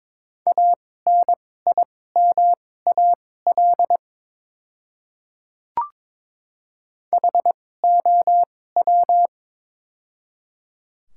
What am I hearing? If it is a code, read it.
Morse code, heard as a keyed tone, about 22 wpm, spaced as for 16 wpm, 700 Hz. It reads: ANIMAL HOW
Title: 2x Character Spacing - Top 500 Words - 22wpm